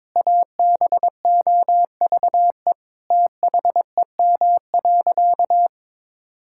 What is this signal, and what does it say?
Morse code, heard as a keyed tone, about 22 wpm, 700 Hz. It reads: ABOVE THEM.
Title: Sentences from Top 1000 Words - Review Part 5 - 22wpm